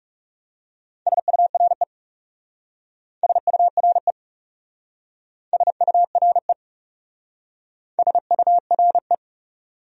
Morse code, heard as a keyed tone, about 36 wpm, 700 Hz. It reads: SURE SURE SURE SURE
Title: Speed-Racing Top 300 Words 30wpm